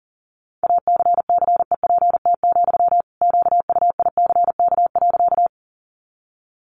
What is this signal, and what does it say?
Morse code, heard as a keyed tone, about 40 wpm, 700 Hz. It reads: ACCEPT, QUICK.
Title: Sentences from Top 1300 Words 40wpm